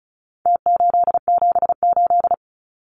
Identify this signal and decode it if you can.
Morse code, heard as a keyed tone, about 34 wpm, 700 Hz. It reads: T878
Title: Numbers - Contest Style - 4-Digit 35wpm